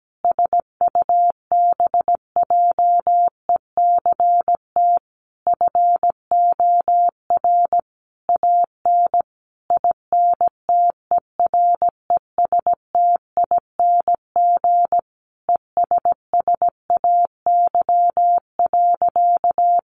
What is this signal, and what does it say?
Morse code, heard as a keyed tone, about 17 wpm, 700 Hz. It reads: SUBJECT FOR AN INTERESTING ESSAY.
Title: Sentences from Top 1500 Words 17wpm